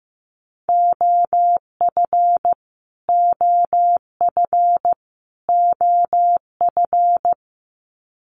Morse code, heard as a keyed tone, about 15 wpm, 700 Hz. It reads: OF OF OF
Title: Top 500 Words Repeated 15wpm